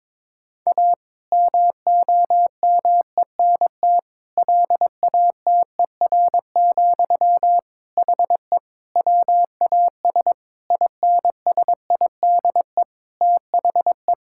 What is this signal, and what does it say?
Morse code, heard as a keyed tone, about 21 wpm, 700 Hz. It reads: A MOMENT LATER, HE WAS INSIDE THE
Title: Sentences from Top 1000 Words - Review Part 6 - 22wpm